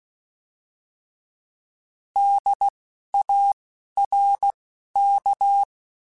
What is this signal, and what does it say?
Morse code, heard as a keyed tone, about 16 wpm, spaced as for 11 wpm, 800 Hz. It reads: DARK
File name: Misc / Misc-mp3s / Word Pairs @10